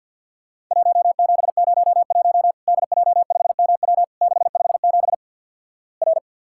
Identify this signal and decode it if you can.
Morse code, heard as a keyed tone, about 50 wpm, 700 Hz. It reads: 1701 DJ5MW 657 R
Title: Worked All Europe DX Contest - QTC Exchange - 50wpm